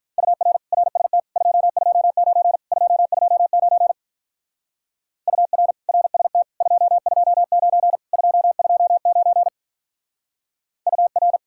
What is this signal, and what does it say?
Morse code, heard as a keyed tone, about 43 wpm, 700 Hz. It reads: UR RST 229 229 UR RST 229 229 UR
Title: Speed-Racing Common QSO Phrases Encore - 35wpm